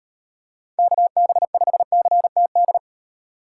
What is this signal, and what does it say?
Morse code, heard as a keyed tone, about 32 wpm, 700 Hz. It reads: KB5CTD